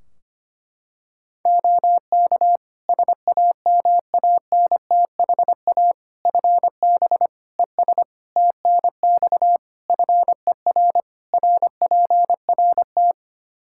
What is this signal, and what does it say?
Morse code, heard as a keyed tone, about 25 wpm, 700 Hz. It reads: OK SAMANTHA FB ES TNX FER RPRT
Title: Practice QSO - Exchange 2 - Info - 25wpm